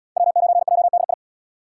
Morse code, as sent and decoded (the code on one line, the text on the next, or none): .-- ----. .--- -.. ..
W9JDI